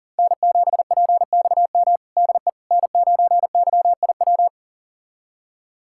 Morse code, heard as a keyed tone, about 40 wpm, 700 Hz. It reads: N7PXM DE N9YIW
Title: USA Callsign DE Callsign - 5 Characters - 40wpm